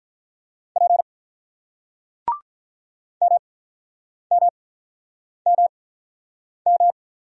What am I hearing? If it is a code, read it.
Morse code, heard as a keyed tone, about 48 wpm, spaced as for 9 wpm, 700 Hz. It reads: P MMMM